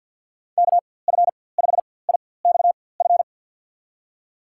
Morse code, heard as a keyed tone, about 53 wpm, spaced as for 24 wpm, 700 Hz. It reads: KF5IXF
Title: USA Call Signs - 6 Characters - 4x Character Spacing - 50wpm